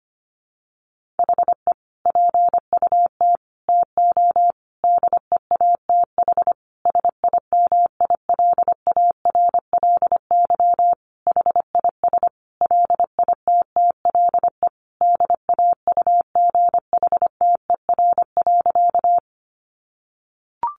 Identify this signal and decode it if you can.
Morse code, heard as a keyed tone, about 25 wpm, 700 Hz. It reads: HE PUT TO DEATH SIMILARLY HIS LITTLE DAUGHTER.
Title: Sentences from Top 1500 Words 25wpm